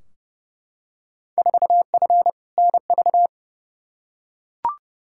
Morse code, heard as a keyed tone, about 30 wpm, 700 Hz. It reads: 4F NV
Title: Field Day Exchanges Encore - 30wpm